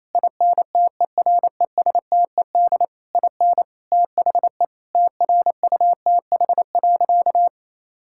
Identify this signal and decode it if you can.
Morse code, heard as a keyed tone, about 28 wpm, 700 Hz. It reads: INTERESTED IN THE TRUTH.
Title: Sentences from Top 1000 Words - Review Part 6 - 28wpm